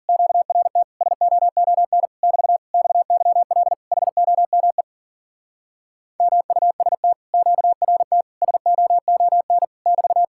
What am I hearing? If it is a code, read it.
Morse code, heard as a keyed tone, about 50 wpm, 700 Hz. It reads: QRT SOON <BT> XYL HOME MUST QRT SOON <BT>
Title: Speed-Racing Common QSO Phrases - 40wpm